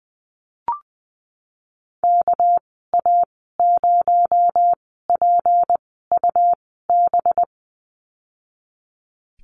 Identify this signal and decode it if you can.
Morse code, heard as a keyed tone, about 20 wpm, spaced as for 15 wpm, 700 Hz. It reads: KA0PUB
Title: 219 Lesson 35g - Number 0 - Call Signs 20wpm